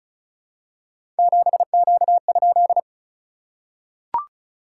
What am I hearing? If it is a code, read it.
Morse code, heard as a keyed tone, about 36 wpm, 690 Hz. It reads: 7Q?
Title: SOTA QSO Practice 35wpm